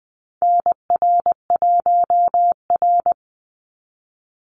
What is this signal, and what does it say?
Morse code, heard as a keyed tone, about 20 wpm, 700 Hz. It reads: NR1R
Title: Call Signs Repeated 2x - 20wpm